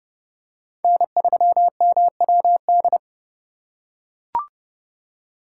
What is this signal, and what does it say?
Morse code, heard as a keyed tone, about 28 wpm, 700 Hz. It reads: N3MWD